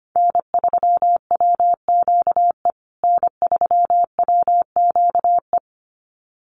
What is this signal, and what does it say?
Morse code, heard as a keyed tone, about 25 wpm, 700 Hz. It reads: N3WQE N3WQE
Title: USA Call Signs Repeated 2x 25wpm